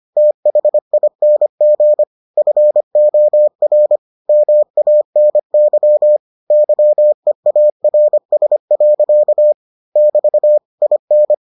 Morse code, heard as a keyed tone, about 25 wpm, 600 Hz. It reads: THING FOR MANY YEARS. <BT> IN